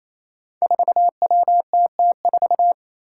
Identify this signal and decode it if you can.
Morse code, heard as a keyed tone, about 28 wpm, 700 Hz. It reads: 4WTT4
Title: Five Challenging Letter-Number Combinations 28wpm